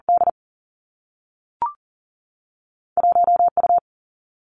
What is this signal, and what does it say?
Morse code, heard as a keyed tone, about 40 wpm, 700 Hz. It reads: D 1 U
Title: Two Challenging Letter-Number Combinations 40wpm